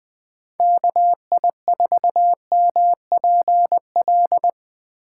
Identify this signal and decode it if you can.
Morse code, heard as a keyed tone, about 20 wpm, 700 Hz. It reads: KI4MPL
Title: USA Call Signs - 6 Characters - 3x Character Spacing - 20wpm